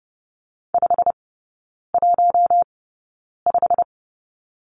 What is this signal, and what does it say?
Morse code, heard as a keyed tone, about 30 wpm, spaced as for 9 wpm, 700 Hz. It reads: 515
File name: course / 30:10wpm / 087 Lesson 15e - Number 5 - Three Characters 30wpm